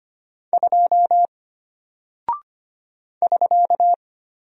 Morse code, heard as a keyed tone, about 25 wpm, 700 Hz. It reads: 2 <SK>
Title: Single Character (Letter:Number:Punctuation:Prosign) Alt 25wpm